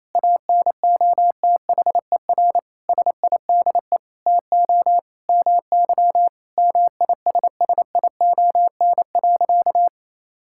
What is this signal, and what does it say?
Morse code, heard as a keyed tone, about 28 wpm, 700 Hz. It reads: ANOTHER SIDE TO MY MISSION.
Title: Sentences from Top 1600 Words 28wpm